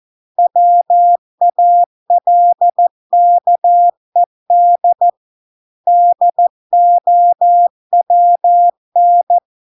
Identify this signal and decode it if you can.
Morse code, heard as a keyed tone, about 14 wpm, 700 Hz. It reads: WALKED DOWN